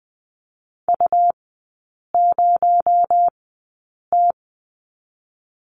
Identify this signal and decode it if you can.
Morse code, heard as a keyed tone, about 20 wpm, spaced as for 8 wpm, 700 Hz. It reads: U0T
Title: 218 Lesson 35e - Number 0 - Three Characters 20wpm